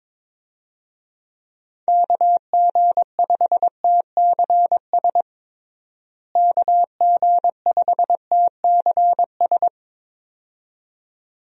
Morse code, heard as a keyed tone, about 21 wpm, 700 Hz. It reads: KG5TCS KG5TCS
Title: USA Call Signs - 6 Characters - Repeated 2x - 22wpm